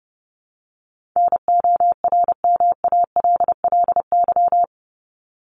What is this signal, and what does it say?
Morse code, heard as a keyed tone, about 30 wpm, 700 Hz. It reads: NORMALLY
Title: LY Suffix - Top 100 Words with Base Words 30wpm